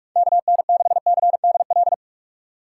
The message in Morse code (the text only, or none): KN6CDL